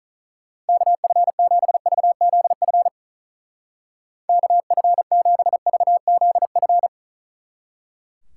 Morse code, heard as a keyed tone, about 45 wpm, 700 Hz. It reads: KF7VZF KF7VZF